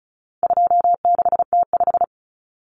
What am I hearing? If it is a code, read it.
Morse code, heard as a keyed tone, about 34 wpm, 700 Hz. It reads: 26T5
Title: Common Cut Numbers - 4-Digit 35wpm